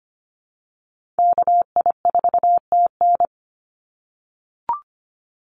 Morse code, heard as a keyed tone, about 25 wpm, 700 Hz. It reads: KI4TN
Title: NA SOTA Chaser Callsigns 25wpm